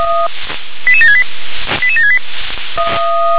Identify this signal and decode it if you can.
Five-tone selective call (ZVEI), heard as quick runs of five tones, tones of about 70 ms. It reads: 88768 88768